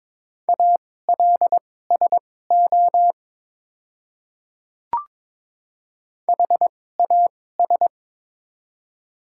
Morse code, heard as a keyed tone, about 22 wpm, spaced as for 16 wpm, 700 Hz. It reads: ALSO HAS
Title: 2x Character Spacing - Top 200 Words - 22wpm